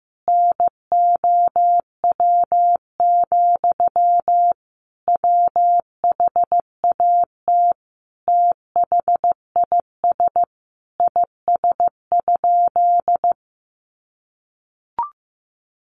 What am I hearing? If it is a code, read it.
Morse code, heard as a keyed tone, about 15 wpm, 700 Hz. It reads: NOW, WHAT THIS IS?
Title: Sentences from Top 1500 Words 15wpm